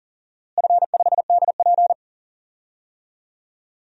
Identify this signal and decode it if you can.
Morse code, heard as a keyed tone, about 40 wpm, 700 Hz. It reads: F5DP
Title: DXCC-100 Call Signs - 4 Characters - Repeated 2x - 40wpm